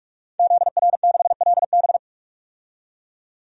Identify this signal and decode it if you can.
Morse code, heard as a keyed tone, about 46 wpm, 700 Hz. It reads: ZR6LB
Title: DXCC-100 Call Signs - 5 Characters - 45wpm